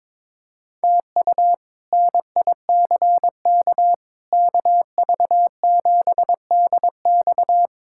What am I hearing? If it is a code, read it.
Morse code, heard as a keyed tone, about 22 wpm, 700 Hz. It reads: TU NICK KV7DX
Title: K1USN SST QSO Practice - 22wpm